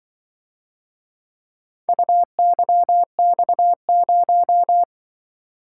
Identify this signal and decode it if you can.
Morse code, heard as a keyed tone, about 25 wpm, 700 Hz. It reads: UYX0
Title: Speed-Racing - Four Letter-Number Combinations - 20wpm